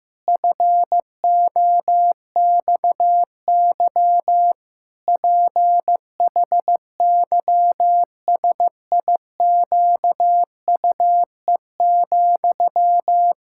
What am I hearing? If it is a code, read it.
Morse code, heard as a keyed tone, about 15 wpm, 700 Hz. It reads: FOXY PHYSIQUE,
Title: Pangram Challenge v2 - 15wpm